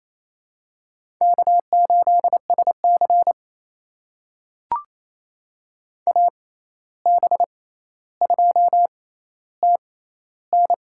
Morse code, heard as a keyed tone, about 28 wpm, spaced as for 13 wpm, 700 Hz. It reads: K8SC AB2TN
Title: USA Call Signs - 6x Character Spacing - 28wpm